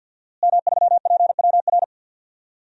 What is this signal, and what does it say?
Morse code, heard as a keyed tone, about 51 wpm, 700 Hz. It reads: M3PWR
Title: DXCC-100 Call Signs - 5 Characters - Repeated 2x - 50wpm